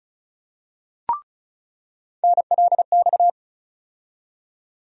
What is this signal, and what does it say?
Morse code, heard as a keyed tone, about 34 wpm, 700 Hz. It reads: NLX
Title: USA Call Sign Postfixes - 3 Letters 35wpm